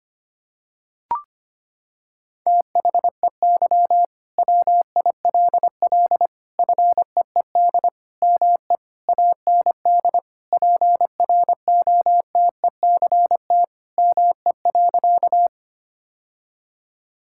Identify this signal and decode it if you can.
Morse code, heard as a keyed tone, about 25 wpm, 700 Hz. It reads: THEY WILL FEED ME AND PROTECT ME.
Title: Sentences from Top 1600 Words 25wpm